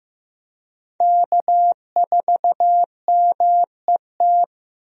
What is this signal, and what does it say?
Morse code, heard as a keyed tone, about 15 wpm, 700 Hz. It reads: K4MET
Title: USA Call Signs - 5 Characters - Repeated 2x - 15wpm